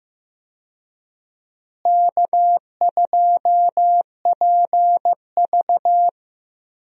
Morse code, heard as a keyed tone, about 15 wpm, 700 Hz. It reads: K2PV
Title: USA Call Signs - 4 Characters - Repeated 3x - 15wpm